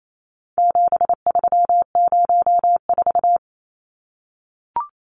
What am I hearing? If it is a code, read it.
Morse code, heard as a keyed tone, about 28 wpm, 700 Hz. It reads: 7304
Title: Numbers - 4-Digit 28wpm